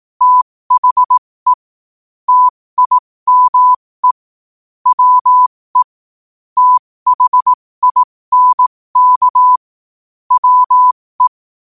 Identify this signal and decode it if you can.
Morse code, heard as a keyed tone, about 18 wpm, spaced as for 15 wpm, 1000 Hz. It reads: THE TIME WE THINK WE
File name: qotdfiles / 15wpm